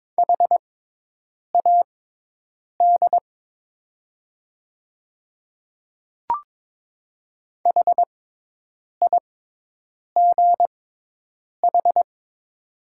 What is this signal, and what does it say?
Morse code, heard as a keyed tone, about 22 wpm, spaced as for 8 wpm, 690 Hz. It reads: HAD HIGH
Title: 6x Character Spacing - Top 300 Words - 22wpm